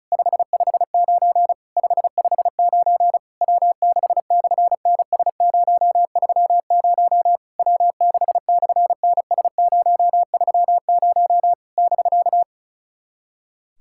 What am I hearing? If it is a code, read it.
Morse code, heard as a keyed tone, about 36 wpm, 700 Hz. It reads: 559 559 W6/NS030 W6/NS030 <BK>
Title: SOTA QSO Practice 35wpm